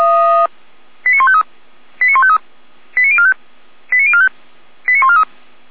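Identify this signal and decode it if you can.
Five-tone selective call (ZVEI), heard as quick runs of five tones, tones of about 70 ms. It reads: 89153 89153 89045 89045 89143
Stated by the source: ZVEI five-tone selective call received by SDR, from feuerwehr-orlamuende (Orlamünde fire brigade) audio archive.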